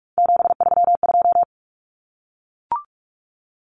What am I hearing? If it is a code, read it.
Morse code, heard as a keyed tone, about 44 wpm, 700 Hz. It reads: 732
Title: Numbers - 3-Digit 45wpm